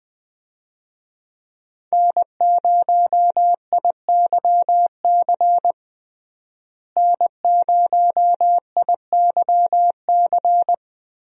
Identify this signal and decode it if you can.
Morse code, heard as a keyed tone, about 21 wpm, 700 Hz. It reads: N0IYC N0IYC
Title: USA Call Signs - 5 Characters - Repeated 3x - 20wpm